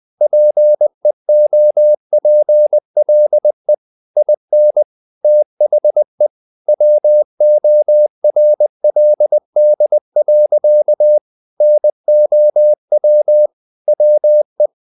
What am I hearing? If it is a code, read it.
Morse code, heard as a keyed tone, about 20 wpm, 600 Hz. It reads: PEOPLE IN THE WORLD. NOW WE
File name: the old man and the sea / the old man and the sea part 1 20wpm